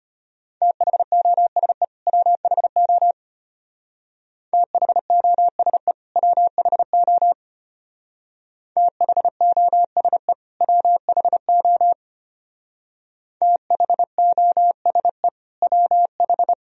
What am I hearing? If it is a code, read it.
Morse code, heard as a keyed tone, about 30 wpm, 700 Hz. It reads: THOSE WHO THOSE WHO THOSE WHO THOSE WH
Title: Speed-Racing - Sets of 2 Words 25wpm